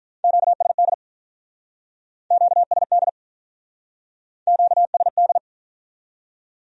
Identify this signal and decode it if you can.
Morse code, heard as a keyed tone, about 45 wpm, 700 Hz. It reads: QSD QSD QSD